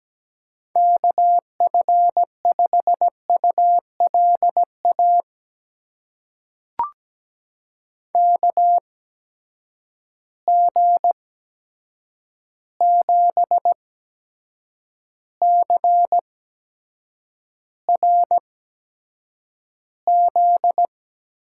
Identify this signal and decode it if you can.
Morse code, heard as a keyed tone, about 17 wpm, spaced as for 7 wpm, 700 Hz. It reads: KF5ULA KG7CRZ
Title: USA Call Signs - 6 Characters - 8x Character Spacing - 17wpm